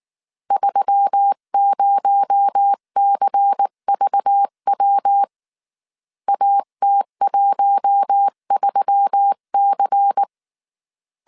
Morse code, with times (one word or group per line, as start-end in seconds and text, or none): 0.5-5.2 30CVW
6.3-10.2 AT13C